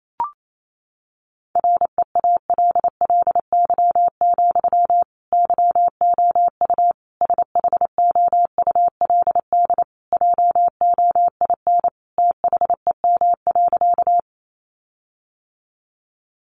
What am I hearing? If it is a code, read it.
Morse code, heard as a keyed tone, about 28 wpm, 700 Hz. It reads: REALLY, YOU SHOULD JOIN THEM.
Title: Sentences from Top 1400 Words 28wpm